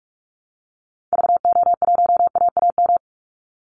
Morse code, heard as a keyed tone, about 44 wpm, 700 Hz. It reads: VO1AAM
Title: Call Signs Repeated 3x - 45wpm